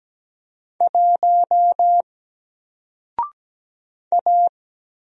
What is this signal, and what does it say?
Morse code, heard as a keyed tone, about 17 wpm, 700 Hz. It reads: 1 A
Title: Single Letter-Number Alt 17wpm